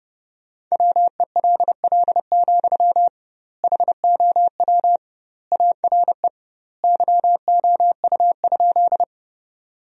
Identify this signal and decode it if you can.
Morse code, heard as a keyed tone, about 32 wpm, 700 Hz. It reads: WELL, HOW ARE YOU?
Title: 2x Word-Spacing - Sentences from Top 200 Words - 30wpm